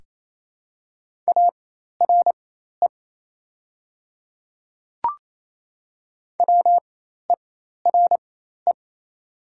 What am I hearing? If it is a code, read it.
Morse code, heard as a keyed tone, about 28 wpm, spaced as for 13 wpm, 700 Hz. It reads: ARE WERE
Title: 4x Character Spacing - Top 100 Words - 28wpm